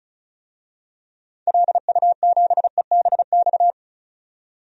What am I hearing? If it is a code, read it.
Morse code, heard as a keyed tone, about 34 wpm, 700 Hz. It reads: LU7EBX
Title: DXCC-100 Call Signs - 6 Characters - 35wpm